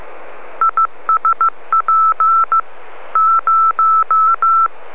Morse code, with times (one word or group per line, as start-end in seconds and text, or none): 0.6-2.6 ISP
3.2-4.7 0